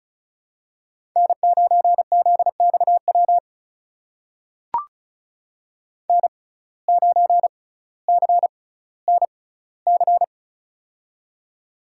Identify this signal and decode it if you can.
Morse code, heard as a keyed tone, about 36 wpm, spaced as for 18 wpm, 700 Hz. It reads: N9ZXW N9CNC
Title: USA Call Signs - 5 Characters - 6x Character Spacing - 35wpm